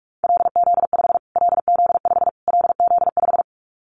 Morse code, heard as a keyed tone, about 44 wpm, 700 Hz. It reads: LZ5 LZ5 LZ5
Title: Callsign Prefixes for Top 25 DXCC - Repeated 3x - 45wpm